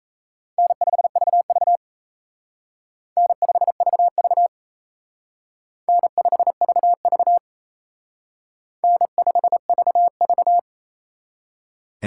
Morse code, heard as a keyed tone, about 35 wpm, 700 Hz. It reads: N5VV N5VV N5VV N5VV